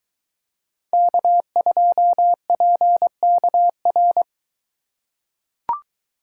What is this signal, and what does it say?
Morse code, heard as a keyed tone, about 21 wpm, 700 Hz. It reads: K2PKR